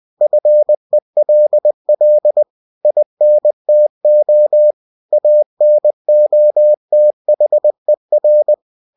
Morse code, heard as a keyed tone, about 20 wpm, 600 Hz. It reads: FELL INTO ANOTHER
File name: cw-books / winnie-the-pooh / winnie-20wpm-page-0066